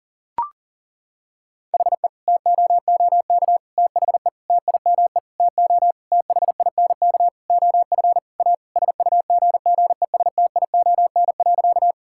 Morse code, heard as a keyed tone, about 38 wpm, 700 Hz. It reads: HE TOOK THE TIME TO THINK OF A SUGGESTION.
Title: Sentences from Top 1600 Words 40wpm